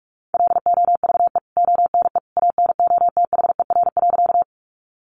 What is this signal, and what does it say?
Morse code, heard as a keyed tone, about 44 wpm, 700 Hz. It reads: LOVE ONE ANOTHER.
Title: Sentences from Top 600 Words 45wpm